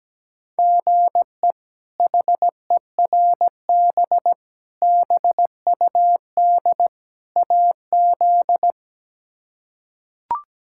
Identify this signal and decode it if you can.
Morse code, heard as a keyed tone, about 17 wpm, 700 Hz. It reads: GE HERB BUD AZ
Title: K1USN SST QSO Practice - 17wpm